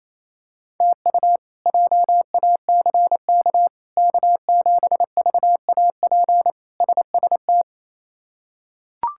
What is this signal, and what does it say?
Morse code, heard as a keyed tone, about 28 wpm, 700 Hz. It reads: TU JACK K7VAP SST
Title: K1USN SST QSO Practice - 28wpm